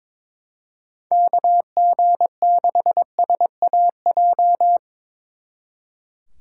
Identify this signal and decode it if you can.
Morse code, heard as a keyed tone, about 22 wpm, 700 Hz. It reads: KG6SAJ
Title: Speed Racing - USA 2x3 Call Signs 22wpm